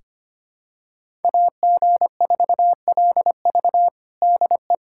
Morse code, heard as a keyed tone, about 25 wpm, 700 Hz. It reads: AG4LV DE
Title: USA Callsign DE Callsign 25wpm